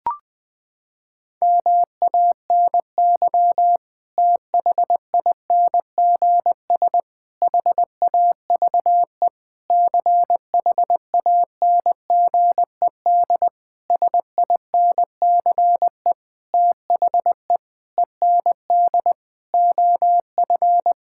Morse code, heard as a keyed tone, about 21 wpm, 700 Hz. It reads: MANY THINGS HAVE CHANGED SINCE THE END OF
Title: Sentences with 7-Word Phrases - 20wpm